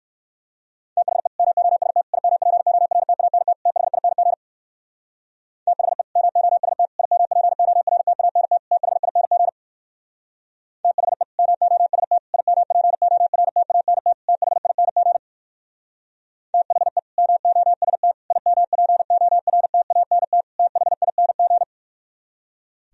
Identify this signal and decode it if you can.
Morse code, heard as a keyed tone, about 62 wpm, 700 Hz. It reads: THE MOST IMPORTANT THING THE MOST IMPORTANT THING THE MOST IMPORTANT THING THE MOST IMPORTANT THING